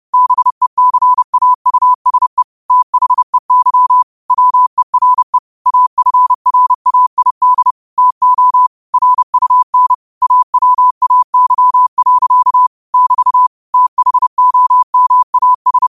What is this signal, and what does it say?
Morse code, heard as a keyed tone, about 30 wpm, 1000 Hz. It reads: BECAUSE THEY WERE AFRAID TO RUN AWAY. <BT> THOMAS
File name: qotdfiles / 30wpm